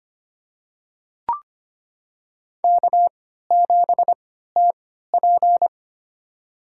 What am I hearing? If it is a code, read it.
Morse code, heard as a keyed tone, about 25 wpm, spaced as for 14 wpm, 700 Hz. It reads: K7TP
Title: USA Call Signs - 4 Characters - 3x Character Spacing - 25wpm